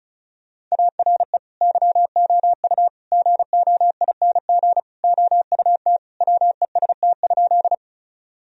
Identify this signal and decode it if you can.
Morse code, heard as a keyed tone, about 36 wpm, 700 Hz. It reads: ARE YOU GOING OUT WEST?
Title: Sentences from Top 1000 Words - Review Part 2 - 35wpm